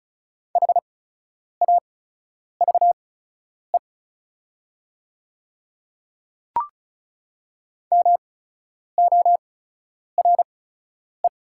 Morse code, heard as a keyed tone, about 36 wpm, spaced as for 10 wpm, 700 Hz. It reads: HAVE MORE
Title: 8x Character Spacing - Top 100 Words - 35wpm